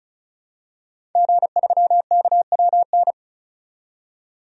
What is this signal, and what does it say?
Morse code, heard as a keyed tone, about 36 wpm, 700 Hz. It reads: G3KWN
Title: DXCC-100 Call Signs - 5 Characters - 35wpm